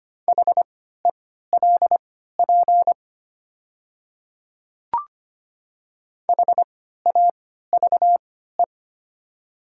Morse code, heard as a keyed tone, about 25 wpm, spaced as for 14 wpm, 700 Hz. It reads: HELP HAVE
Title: 3x Character Spacing - Top 200 Words - 25wpm